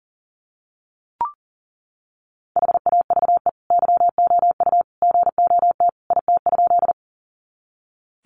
Morse code, heard as a keyed tone, about 38 wpm, 700 Hz. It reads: HAVE YOU GOT IT?
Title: Sentences from Top 300 Words 40wpm